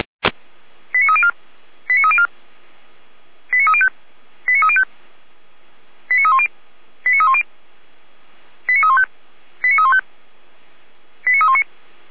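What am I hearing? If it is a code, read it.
Five-tone selective call (ZVEI), heard as quick runs of five tones, tones of about 70 ms. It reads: 89304 89304 89305 89305 89310 89310 89315 89315 89319
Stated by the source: ZVEI five-tone selective call received by SDR.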